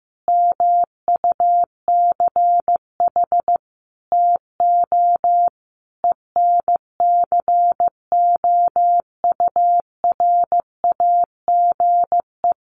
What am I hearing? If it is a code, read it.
Morse code, heard as a keyed tone, about 15 wpm, 700 Hz. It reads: MUCH TO ENCOURAGE
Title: Sentences from Top 1400 Words 15wpm